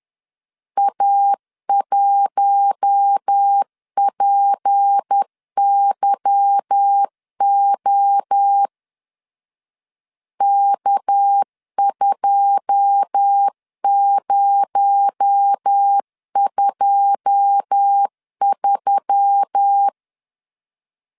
Morse code, heard as a keyed tone, about 10 wpm, 800 Hz. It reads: A1PYO K2023
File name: LEZIONI / 9 2 10wpm 2736